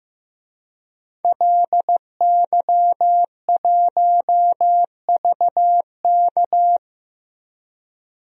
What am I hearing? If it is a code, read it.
Morse code, heard as a keyed tone, about 15 wpm, 700 Hz. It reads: LY1VK
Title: DXCC-100 Call Signs - 5 Characters - 15wpm